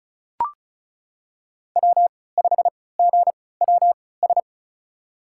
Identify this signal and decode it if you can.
Morse code, heard as a keyed tone, about 36 wpm, spaced as for 20 wpm, 690 Hz. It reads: W5GWS
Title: USA Call Signs - 5 Characters - 3x Character Spacing - 35wpm